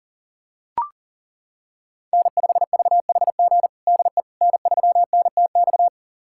Morse code, heard as a keyed tone, about 40 wpm, 700 Hz. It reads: N5VHG DE N3NTX
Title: USA Callsign DE Callsign - 5 Characters - 40wpm